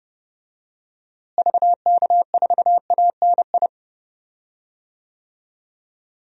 Morse code, heard as a keyed tone, about 30 wpm, 700 Hz. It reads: VK4ANI
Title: DXCC-100 Call Signs - 6 Characters - Repeated 2x - 30wpm